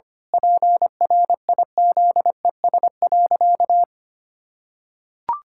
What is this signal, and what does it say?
Morse code, heard as a keyed tone, about 25 wpm, 700 Hz. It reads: PRIZES.
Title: Pangram Challenge v2 - 25wpm